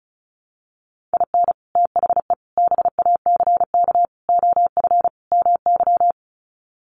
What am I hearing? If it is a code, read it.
Morse code, heard as a keyed tone, about 34 wpm, 700 Hz. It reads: IN THE BACK OF MY
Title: Sets of 5 Words Encore 35wpm